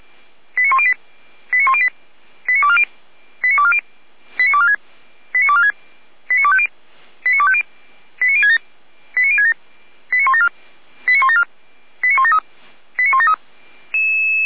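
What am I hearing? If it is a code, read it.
Five-tone selective call (ZVEI), heard as quick runs of five tones, tones of about 70 ms. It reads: 89108 89108 89240 89240 89246 89246 89250 89250 89067 89067 89174 89174 89173 89173
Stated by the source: Five-tone ZVEI selective call 89337 received by SDR, from the audio server of Feuerwehr Orlamünde.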